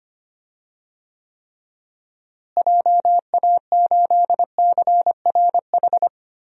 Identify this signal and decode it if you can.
Morse code, heard as a keyed tone, about 25 wpm, 700 Hz. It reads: JA8CRH
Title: DXCC-100 Call Signs - 6 Characters - Repeated 3x - 25wpm